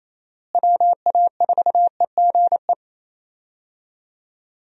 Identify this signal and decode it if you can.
Morse code, heard as a keyed tone, about 28 wpm, 700 Hz. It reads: WA4EGE
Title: USA Call Signs - 6 Characters - Repeated 3x - 28wpm